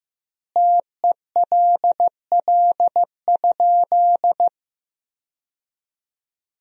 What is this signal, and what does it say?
Morse code, heard as a keyed tone, about 15 wpm, 700 Hz. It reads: TELL?
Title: 2x Word-Spacing - Sentences from Top 300 Words - 15wpm